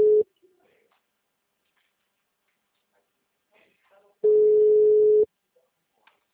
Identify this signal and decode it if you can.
Ringback tone (Europe, 425 Hz).
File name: out- -2021- - - .5282